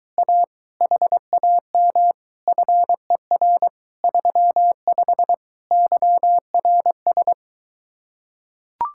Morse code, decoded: A HAM FER 35 YRS

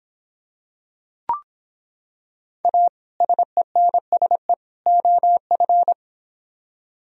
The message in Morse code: A SENSE OF

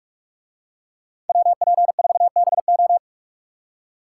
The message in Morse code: WP4BO

